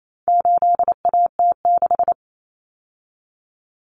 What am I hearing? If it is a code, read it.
Morse code, heard as a keyed tone, about 28 wpm, 700 Hz. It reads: 8AT6
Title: Common Cut Numbers - 4-Digit 28wpm